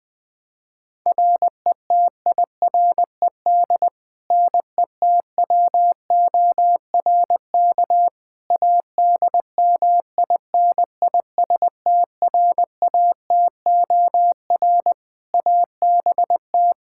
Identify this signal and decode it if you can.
Morse code, heard as a keyed tone, about 20 wpm, 700 Hz. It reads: RETIRED NETWORK ADMINISTRATOR ABT
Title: Practice QSO - Exchange 3 - More Info - 20wpm